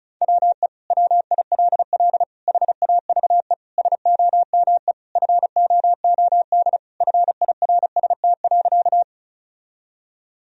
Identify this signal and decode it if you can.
Morse code, heard as a keyed tone, about 36 wpm, 700 Hz. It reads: WE WILL HAVE SOME FOOD FIRST.
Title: Sentences from Top 1000 Words - Review Part 1 - 35wpm